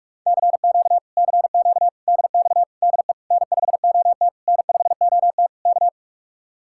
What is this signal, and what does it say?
Morse code, heard as a keyed tone, about 50 wpm, 700 Hz. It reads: CQ CQ DX DE N5OT N5OT K